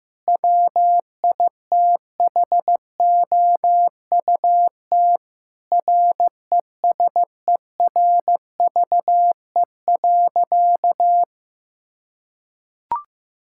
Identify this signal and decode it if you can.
Morse code, heard as a keyed tone, about 15 wpm, 700 Hz. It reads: WITHOUT RESERVE.
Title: Sentences from Top 1000 Words - Review Part 6 - 15wpm